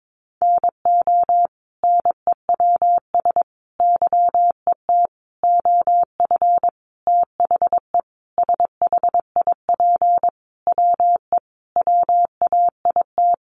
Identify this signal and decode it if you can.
Morse code, heard as a keyed tone, about 22 wpm, 700 Hz. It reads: NO NEWS YET OF THE SHIP WE WAIT
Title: Sentences from Top 800 Words 22wpm